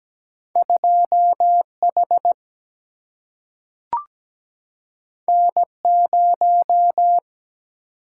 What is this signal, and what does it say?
Morse code, heard as a keyed tone, about 17 wpm, 700 Hz. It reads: 2 H N 0
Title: Single-Letter Single-Number 17wpm